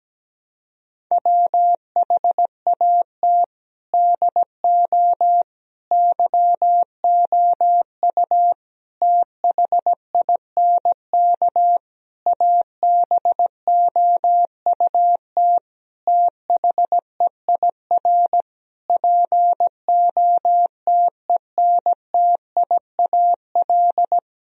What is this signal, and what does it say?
Morse code, heard as a keyed tone, about 17 wpm, 700 Hz. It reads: WHAT DO YOU THINK ABOUT THEIR POTENTIAL
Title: Sentences from Top 1000 Words - Review Part 4 - 17wpm